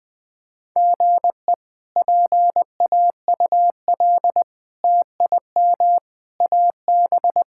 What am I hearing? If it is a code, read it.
Morse code, heard as a keyed tone, about 20 wpm, 700 Hz. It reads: GE PAUL TIM AB
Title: K1USN SST QSO Practice - 20wpm